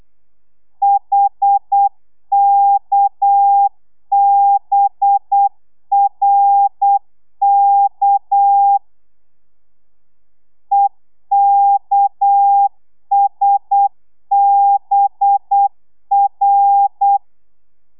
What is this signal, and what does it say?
Morse code, heard as a keyed tone, about 8 wpm, 800 Hz. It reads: HKBRK EKSBR
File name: telegrafia / i2tfj / Lez6 35